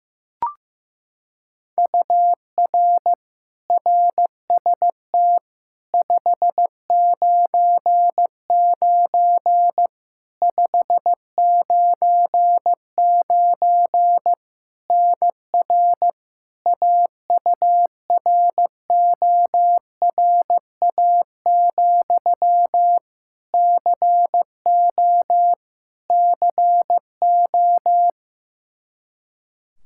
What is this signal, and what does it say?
Morse code, heard as a keyed tone, about 16 wpm, 700 Hz. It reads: UR RST 599 599 NR AURORA, CO CO
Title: K1USN SST QSO Practice 15wpm